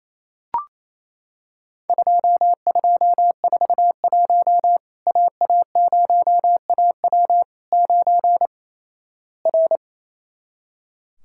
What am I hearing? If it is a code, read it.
Morse code, heard as a keyed tone, about 27 wpm, 700 Hz. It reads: 2241 AA0AW 9 R
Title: Worked All Europe DX Contest - QTC Exchange - 28wpm